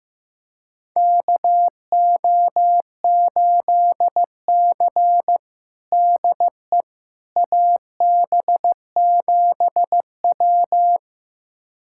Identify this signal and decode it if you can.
Morse code, heard as a keyed tone, about 15 wpm, 700 Hz. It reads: KO8C DE AB7W
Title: USA Callsign DE Callsign - 4 Characters - 15wpm